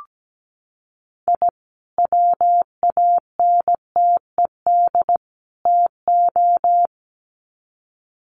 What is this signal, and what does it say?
Morse code, heard as a keyed tone, about 17 wpm, 700 Hz. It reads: I WANTED TO
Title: Sets of 3 Words 17wpm